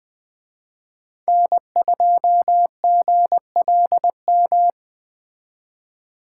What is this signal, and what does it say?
Morse code, heard as a keyed tone, about 19 wpm, 700 Hz. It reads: N2GLM